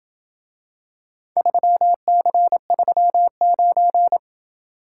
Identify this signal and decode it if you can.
Morse code, heard as a keyed tone, about 27 wpm, 700 Hz. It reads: 3C39